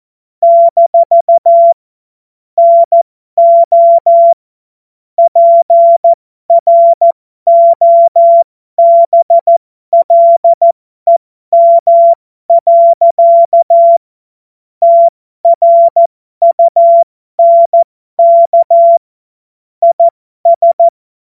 Morse code, heard as a keyed tone, about 14 wpm, spaced as for 12 wpm, 680 Hz. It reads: - NO PROBLEM. TRUNK IS